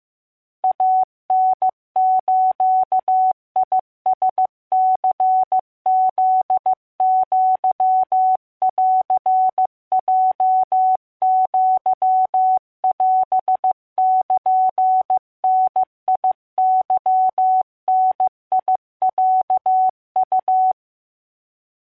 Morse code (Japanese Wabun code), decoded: イタズラニフアンヲアオルダケダロウ